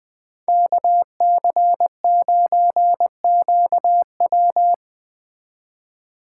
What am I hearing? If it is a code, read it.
Morse code, heard as a keyed tone, about 20 wpm, 700 Hz. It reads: KC9QW
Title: Speed Racing - USA 2x2 Call Signs 17wpm